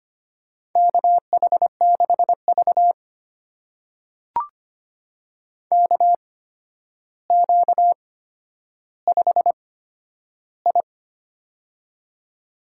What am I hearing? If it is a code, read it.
Morse code, heard as a keyed tone, about 25 wpm, spaced as for 11 wpm, 700 Hz. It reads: KH6V KQ5I
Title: USA Call Signs - 8x Character Spacing - 25wpm